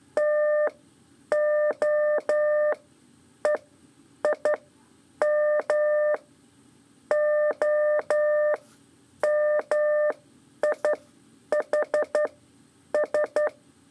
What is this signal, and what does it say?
Morse code, heard as a keyed tone, about 11 wpm, spaced as for 8 wpm, 600 Hz. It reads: TOEIM OMIHS
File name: 3 lezione CW 2 AIR